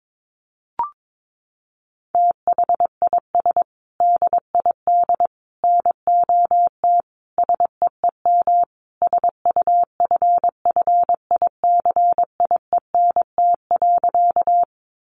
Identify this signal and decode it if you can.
Morse code, heard as a keyed tone, about 22 wpm, 700 Hz. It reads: THIS DID NOT SEEM SUFFICIENT.
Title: Sentences from Top 1400 Words 22wpm